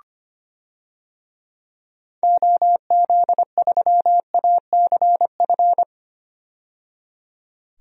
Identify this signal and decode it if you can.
Morse code, heard as a keyed tone, about 25 wpm, 700 Hz. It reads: OZ3ACF
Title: DXCC-100 Call Signs - 6 Characters - 25wpm